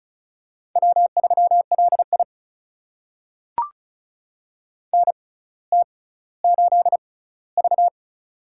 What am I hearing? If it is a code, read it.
Morse code, heard as a keyed tone, about 36 wpm, spaced as for 18 wpm, 700 Hz. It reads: W3LI NT8V
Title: USA Call Signs - 4 Characters - 6x Character Spacing - 35wpm